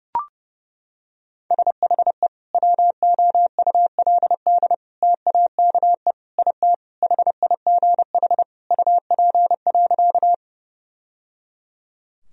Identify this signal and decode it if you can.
Morse code, heard as a keyed tone, about 31 wpm, 700 Hz. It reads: SHE WOULD TAKE IT HIGH UP.
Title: Sentences from Top 1000 Words - Review Part 1 - 30wpm